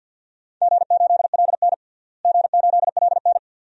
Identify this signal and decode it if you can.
Morse code, heard as a keyed tone, about 53 wpm, 700 Hz. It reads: G8LN G8LN
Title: DXCC-100 Call Signs - 4 Characters - Repeated 2x - 50wpm